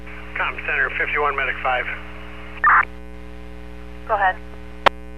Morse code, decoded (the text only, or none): none